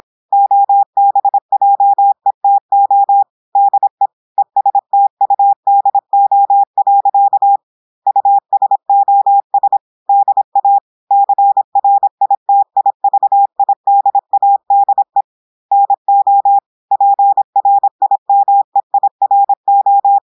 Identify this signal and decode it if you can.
Morse code, heard as a keyed tone, about 26 wpm, 800 Hz. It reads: OBJETO DE ESTUDO. USOS DA CRITIVIDADE NO PRIMEIRO
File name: audios / portugais / portugais texte en clair 1 vitesse 26 mots minutes